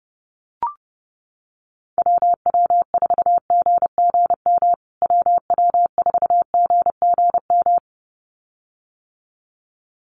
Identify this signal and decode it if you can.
Morse code, heard as a keyed tone, about 30 wpm, 700 Hz. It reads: WW4GGM WW4GGM
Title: USA Call Signs Repeated 2x 30wpm